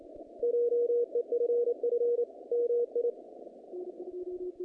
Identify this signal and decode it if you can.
Morse code, heard as a keyed tone, about 29 wpm, 480 Hz. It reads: JEFF MI
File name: contests / naqp-cw-k5tr-op-k5pi- -2300